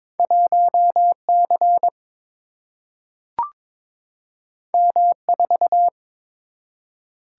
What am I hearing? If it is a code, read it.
Morse code, heard as a keyed tone, about 22 wpm, 700 Hz. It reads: 1 C M 4
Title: Single-Letter Single-Number 22wpm